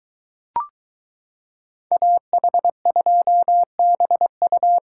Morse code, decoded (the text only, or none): AH2BU